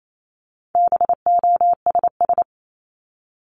Morse code, heard as a keyed tone, about 28 wpm, 700 Hz. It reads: BOSS
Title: Top 501-750 Words 28wpm